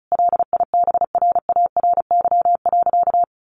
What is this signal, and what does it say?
Morse code, heard as a keyed tone, about 34 wpm, 700 Hz. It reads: LIBRARY.
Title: Sentences from Top 1300 Words 35wpm